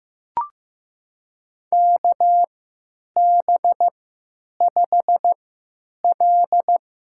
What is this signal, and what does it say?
Morse code, heard as a keyed tone, about 15 wpm, spaced as for 9 wpm, 700 Hz. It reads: KB5L
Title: USA Call Signs - 3x Character Spacing - 15wpm